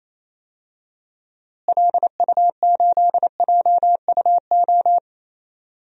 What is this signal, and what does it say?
Morse code, heard as a keyed tone, about 28 wpm, 700 Hz. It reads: LU8JUO
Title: DXCC-100 Call Signs - 6 Characters - Repeated 3x - 28wpm